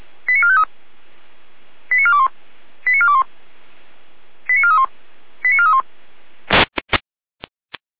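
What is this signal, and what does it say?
Five-tone selective call (ZVEI), heard as quick runs of five tones, tones of about 70 ms. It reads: 89453 89421 89421 89431 89431
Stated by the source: ZVEI five-tone selective call received by SDR.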